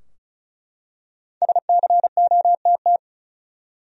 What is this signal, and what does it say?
Morse code, heard as a keyed tone, about 34 wpm, 700 Hz. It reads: SCOTT
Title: USA Common Names - 35wpm